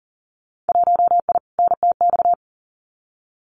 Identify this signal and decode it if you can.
Morse code, heard as a keyed tone, about 40 wpm, 700 Hz. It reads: 1I NTX
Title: Winter Field Day Exchanges Encore 40wpm